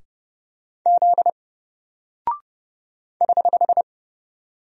Morse code, heard as a keyed tone, about 30 wpm, 700 Hz. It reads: Z <HH>
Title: Single Character (Letter:Number:Punctuation:Prosign) 30wpm